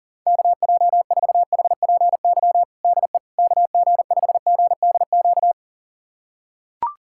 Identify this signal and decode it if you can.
Morse code, heard as a keyed tone, about 40 wpm, 700 Hz. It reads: KJ4HPY DE KG5GDQ
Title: USA Callsign DE Callsign 40wpm